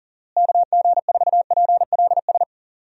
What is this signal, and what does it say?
Morse code, heard as a keyed tone, about 40 wpm, 700 Hz. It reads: KG4PLS
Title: USA Call Signs - 6 Characters - Repeated 3x - 40wpm